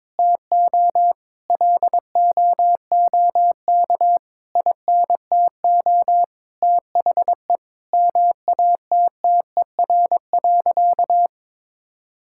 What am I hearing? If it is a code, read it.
Morse code, heard as a keyed tone, about 21 wpm, 700 Hz. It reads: TO LOOK INTO THE MATTER.
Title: Sentences from Top 1000 Words - Review Part 6 - 22wpm